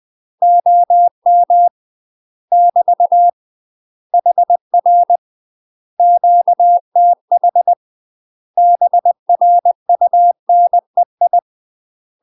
Morse code, heard as a keyed tone, about 20 wpm, 700 Hz. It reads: OM <BT> HR QTH BRUNEI